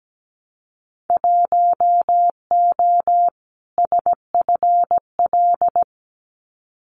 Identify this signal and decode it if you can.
Morse code, heard as a keyed tone, about 17 wpm, 700 Hz. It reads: 1O SFL
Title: Winter Field Day Exchanges 17wpm